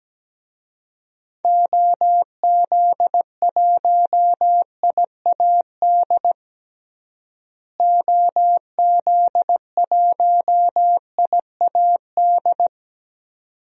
Morse code, heard as a keyed tone, about 17 wpm, 700 Hz. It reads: OZ1IAD OZ1IAD
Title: DXCC-100 Call Signs - 6 Characters - Repeated 2x - 17wpm